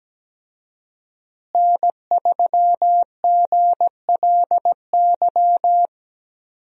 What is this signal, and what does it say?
Morse code, heard as a keyed tone, about 17 wpm, 700 Hz. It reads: N3GLY
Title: USA Call Signs - 5 Characters - Repeated 3x - 17wpm